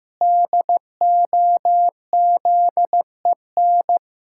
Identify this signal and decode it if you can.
Morse code, heard as a keyed tone, about 14 wpm, 700 Hz. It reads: DOZEN
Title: Pangram Challenge v2 - 15wpm